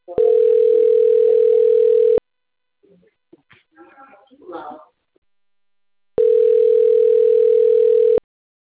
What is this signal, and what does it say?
Ringback tone (North America), 440 + 480 Hz, with speech.
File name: rg-1600-unknown- - - .34516